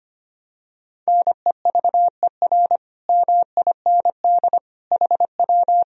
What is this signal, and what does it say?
Morse code, heard as a keyed tone, about 25 wpm, 700 Hz. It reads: NEVER MIND HW
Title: Sentences from Top 500 Words and Abbreviations 25wpm